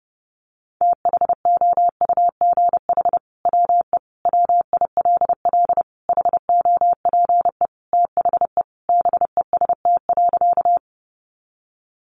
Morse code, heard as a keyed tone, about 30 wpm, 700 Hz. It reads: THOUGH WE WILL HOPE THE BEST.
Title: Sentences from Top 1000 Words 30wpm